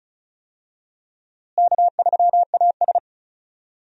Morse code, heard as a keyed tone, about 34 wpm, 700 Hz. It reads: K3AS
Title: USA Call Signs - 4 Characters - Repeated 2x - 35wpm